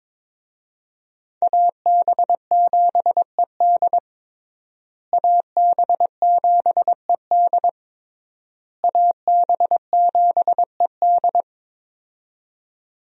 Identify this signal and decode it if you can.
Morse code, heard as a keyed tone, about 22 wpm, 700 Hz. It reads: AB7ED AB7ED AB7ED
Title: USA Call Signs - 5 Characters - Repeated 3x - 22wpm